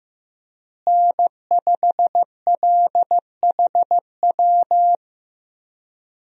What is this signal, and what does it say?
Morse code, heard as a keyed tone, about 15 wpm, 700 Hz. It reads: N5LHW
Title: USA Call Signs - 5 Characters - Repeated 3x - 15wpm